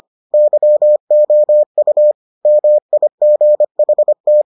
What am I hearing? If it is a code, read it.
Morse code, heard as a keyed tone, about 25 wpm, 600 Hz. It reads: YOU MIGHT